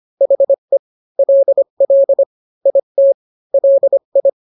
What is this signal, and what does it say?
Morse code, heard as a keyed tone, about 25 wpm, 550 Hz. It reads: HE LL IT LI